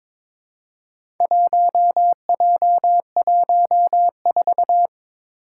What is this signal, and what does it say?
Morse code, heard as a keyed tone, about 22 wpm, 700 Hz. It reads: 1J14
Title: Four Challenging Letter-Number Combinations 22wpm